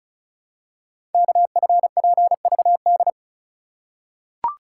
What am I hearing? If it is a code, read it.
Morse code, heard as a keyed tone, about 36 wpm, 700 Hz. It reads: KFPVD
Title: Code Groups - 5 Letters - 35wpm